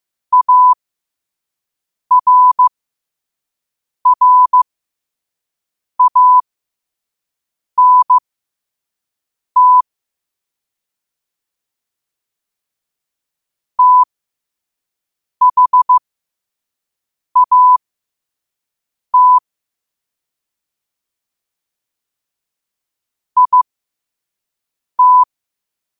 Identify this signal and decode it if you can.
Morse code, heard as a keyed tone, about 15 wpm, spaced as for 5 wpm, 1000 Hz. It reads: ARRANT THAT IT
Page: New quotes every day in morse code at 5 Words per minute.